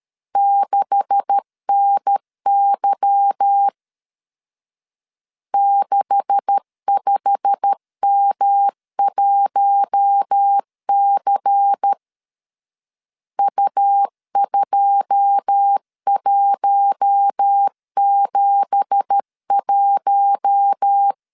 Morse code, heard as a keyed tone, about 13 wpm, 800 Hz. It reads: 6NY 65M1C U2171
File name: CW / DET / 1 9 1500C S B 12wpm 3819